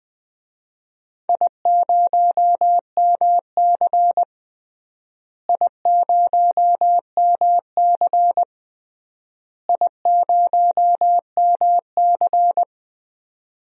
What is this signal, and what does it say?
Morse code, heard as a keyed tone, about 20 wpm, 700 Hz. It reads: I0MC I0MC I0MC
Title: DXCC-100 Call Signs - 4 Characters - Repeated 3x - 20wpm